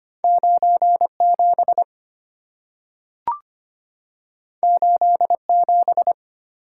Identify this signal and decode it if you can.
Morse code, heard as a keyed tone, about 25 wpm, 700 Hz. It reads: 9 7 8 7
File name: Two Letter-Number Combinations - 25wpm